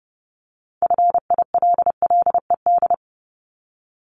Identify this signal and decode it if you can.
Morse code, heard as a keyed tone, about 30 wpm, 700 Hz. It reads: FILLED
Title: Multi-letter Phonemes 30wpm